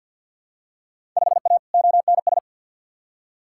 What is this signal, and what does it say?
Morse code, heard as a keyed tone, about 52 wpm, 700 Hz. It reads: 5A ONS
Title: Field Day Exchanges Encore - 50wpm